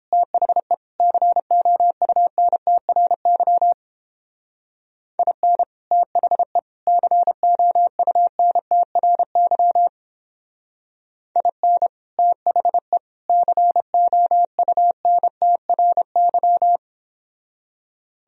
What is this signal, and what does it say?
Morse code, heard as a keyed tone, about 31 wpm, 700 Hz. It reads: THE COUNTRY IN THE COUNTRY IN THE COUNTRY